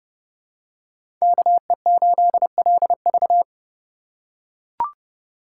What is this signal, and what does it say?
Morse code, heard as a keyed tone, about 30 wpm, 700 Hz. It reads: KE8LV
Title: USA Call Signs - 5 Characters - 3x Character Spacing - 30wpm